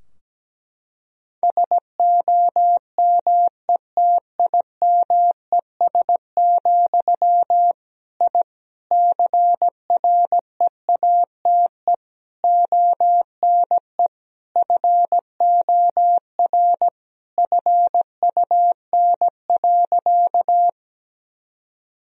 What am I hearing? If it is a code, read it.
Morse code, heard as a keyed tone, about 17 wpm, 700 Hz. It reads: SOMETIMES, I CREATE ONE FOR FUN.
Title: Sentences from Top 1000 Words - Review Part 3 - 17wpm